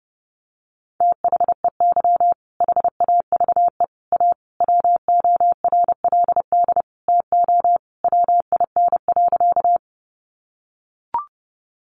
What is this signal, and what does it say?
Morse code, heard as a keyed tone, about 30 wpm, 700 Hz. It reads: THEY HAVE A WORLD TO WIN.
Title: Sentences from Top 1500 Words 30wpm